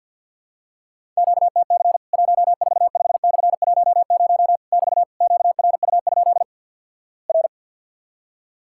Morse code, heard as a keyed tone, about 48 wpm, 700 Hz. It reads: QTC 145/10 <BT> QRU? R
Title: Worked All Europe DX Contest - QTC Exchange with Repeats - 50wpm